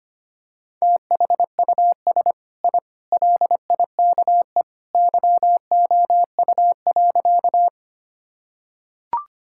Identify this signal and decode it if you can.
Morse code, heard as a keyed tone, about 25 wpm, 700 Hz. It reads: THUS I LIKE YOU.
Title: Sentences from Top 1000 Words - Review Part 5 - 25wpm